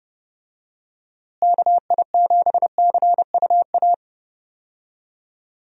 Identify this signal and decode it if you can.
Morse code, heard as a keyed tone, about 30 wpm, 700 Hz. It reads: KI7CUA
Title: USA Call Signs - 6 Characters - Repeated 2x - 30wpm